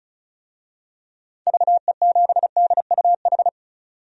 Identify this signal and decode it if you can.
Morse code, heard as a keyed tone, about 36 wpm, 700 Hz. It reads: VE7DUH
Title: DXCC-100 Call Signs - 6 Characters - 35wpm